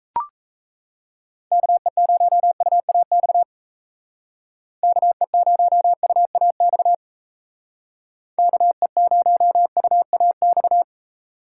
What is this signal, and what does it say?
Morse code, heard as a keyed tone, about 35 wpm, 700 Hz. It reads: KE0UAX KE0UAX KE0UAX